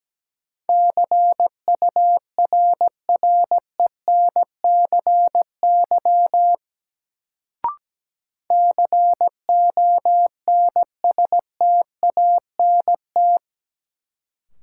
Morse code, heard as a keyed tone, about 17 wpm, 700 Hz. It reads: CURRENCY CONSTANT
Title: Sentences from Top 1500 Words - New Word Drill - 17wpm